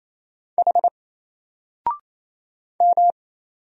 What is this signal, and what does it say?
Morse code, heard as a keyed tone, about 27 wpm, 700 Hz. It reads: H M
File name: Single-Letter-Number-28wpm